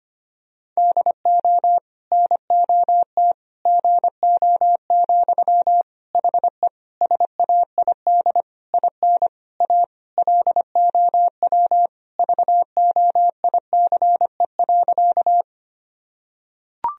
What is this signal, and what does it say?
Morse code, heard as a keyed tone, about 25 wpm, 700 Hz. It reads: DO NOT GO, HE SAID IN A LOW VOICE.
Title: Sentences from Top 1000 Words - Review Part 1 - 25wpm